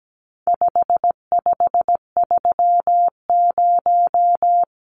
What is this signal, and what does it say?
Morse code, heard as a keyed tone, about 17 wpm, 700 Hz. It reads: 5530
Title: Numbers - 4-Digit 17wpm